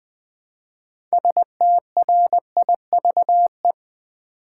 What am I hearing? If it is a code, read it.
Morse code, heard as a keyed tone, about 20 wpm, 700 Hz. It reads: STRIVE
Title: Build A Word - 3 Letters At A Time - 6-Letter Words - 20wpm